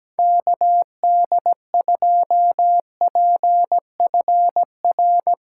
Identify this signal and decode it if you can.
Morse code, heard as a keyed tone, about 17 wpm, 700 Hz. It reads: KD2PFR
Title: USA Callsign DE Callsign - 6 Characters - 17wpm